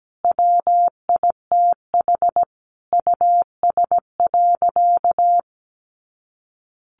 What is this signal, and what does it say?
Morse code, heard as a keyed tone, about 17 wpm, 700 Hz. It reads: WITH US.
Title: A Sound Of Thunder 17wpm